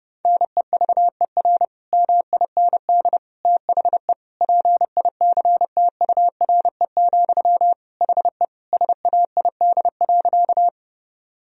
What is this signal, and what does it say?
Morse code, heard as a keyed tone, about 32 wpm, 700 Hz. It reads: NEVER MIND THE PICTURE, HE SAID.
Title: Sentences from Top 1000 Words - Review Part 1 - 30wpm